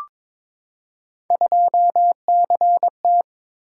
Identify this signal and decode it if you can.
Morse code, heard as a keyed tone, about 22 wpm, 700 Hz. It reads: 2CT
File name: practice / 22wpm / Three Letter-Number Combinations - 22wpm